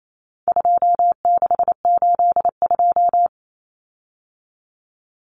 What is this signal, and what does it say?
Morse code, heard as a keyed tone, about 28 wpm, 700 Hz. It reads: 2682
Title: Common Cut Numbers - 4-Digit 28wpm